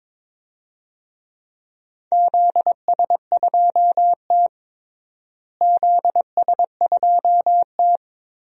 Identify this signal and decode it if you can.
Morse code, heard as a keyed tone, about 22 wpm, 700 Hz. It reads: ZS2T ZS2T
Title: DXCC-100 Call Signs - 4 Characters - Repeated 3x - 22wpm